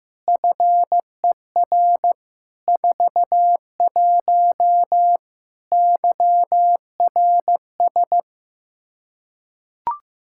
Morse code, heard as a keyed tone, about 15 wpm, 700 Hz. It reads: FER 41 YRS
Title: Practice QSO - Exchange 3 - More Info - 15wpm